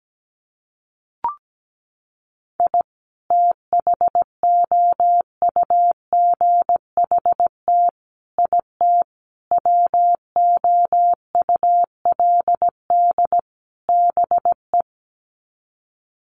Morse code, heard as a keyed tone, about 17 wpm, 700 Hz. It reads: I THOUGHT IT WOULD BE
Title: Sets of 5 Words Encore 17wpm